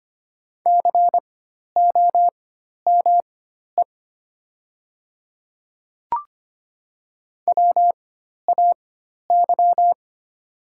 Morse code, heard as a keyed tone, about 25 wpm, spaced as for 12 wpm, 700 Hz. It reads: COME WAY
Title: 4x Character Spacing - Top 100 Words - 25wpm